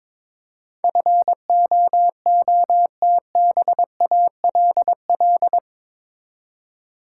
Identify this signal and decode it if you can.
Morse code, heard as a keyed tone, about 22 wpm, 700 Hz. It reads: FOOTBALL
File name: Difficult Words with Hints - 22wpm